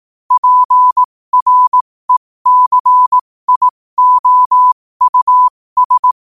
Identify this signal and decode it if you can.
Morse code, heard as a keyed tone, about 18 wpm, spaced as for 15 wpm, 1000 Hz. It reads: PRECIOUS
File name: qotdfiles / 15wpm